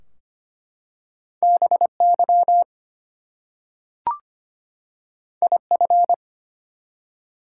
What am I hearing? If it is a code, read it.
Morse code, heard as a keyed tone, about 25 wpm, 700 Hz. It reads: BY IF
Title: Two Letter Words - 25wpm